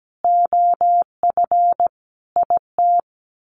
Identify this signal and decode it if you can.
Morse code, heard as a keyed tone, about 17 wpm, 700 Hz. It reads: OF IT
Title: Sets of 5 Words Encore 17wpm